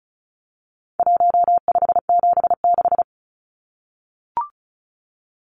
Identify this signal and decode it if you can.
Morse code, heard as a keyed tone, about 34 wpm, 700 Hz. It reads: 1576
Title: Numbers - 4-Digit 35wpm